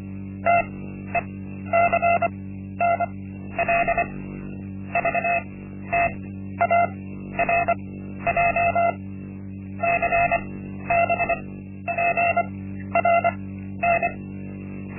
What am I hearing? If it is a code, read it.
Morse code, heard as a keyed tone, about 25 wpm, spaced as for 12 wpm, 690 Hz. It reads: TECNL VTARJ CBPRN
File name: Linksmakalnis-1